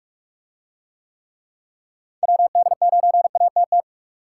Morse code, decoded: WD9ATT

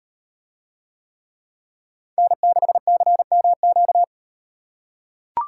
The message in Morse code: N6CMQ

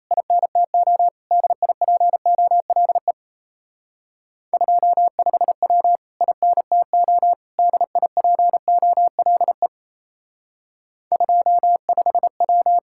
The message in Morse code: INTO DIPOLE 25W INTO DIPOLE 25W

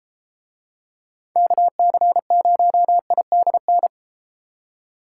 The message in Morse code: KC0IDN

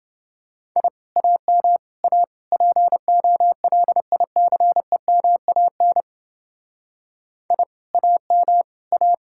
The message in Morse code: I AM A POLICEMAN I AM A

